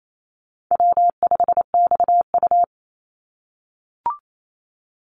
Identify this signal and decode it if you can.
Morse code, heard as a keyed tone, about 28 wpm, 700 Hz. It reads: W5XU
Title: Easy Call Signs - 28wpm